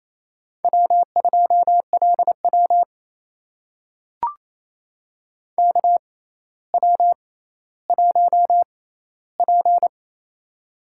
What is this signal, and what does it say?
Morse code, heard as a keyed tone, about 28 wpm, spaced as for 14 wpm, 700 Hz. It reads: W2LW KW1P
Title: USA Call Signs - 4 Characters - 6x Character Spacing - 28wpm